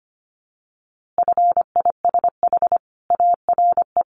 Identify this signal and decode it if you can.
Morse code, heard as a keyed tone, about 25 wpm, 700 Hz. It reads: FISH ARE
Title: Easy Phrases 25wpm